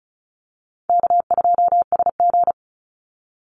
Morse code, heard as a keyed tone, about 34 wpm, 700 Hz. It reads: K2SG
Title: Call Signs Repeated 3x - 35wpm